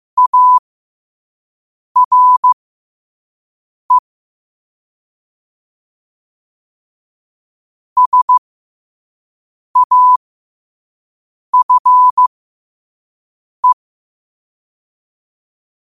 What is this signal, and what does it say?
Morse code, heard as a keyed tone, about 15 wpm, spaced as for 5 wpm, 1000 Hz. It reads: ARE SAFE
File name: 5wpm